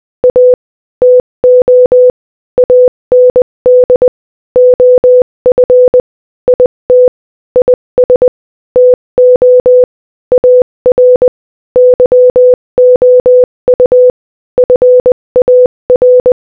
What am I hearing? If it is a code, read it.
Morse code, heard as a keyed tone, about 20 wpm, 500 Hz. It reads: A TO AND OF IT IS TO AR YOU FAR
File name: Morse-Code-Audio-Speed-20-14